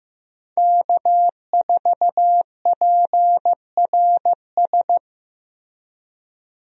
Morse code, heard as a keyed tone, about 15 wpm, 700 Hz. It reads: K4PRS
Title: USA Call Signs - 5 Characters - Repeated 3x - 15wpm